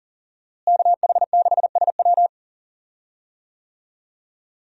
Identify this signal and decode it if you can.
Morse code, heard as a keyed tone, about 40 wpm, 700 Hz. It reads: KH6SW
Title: DXCC-100 Call Signs - 5 Characters - Repeated 2x - 40wpm